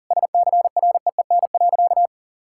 Morse code, heard as a keyed tone, about 40 wpm, 700 Hz. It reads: SCREEN.
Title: Sentences from Top 1000 Words - Review Part 6 - 40wpm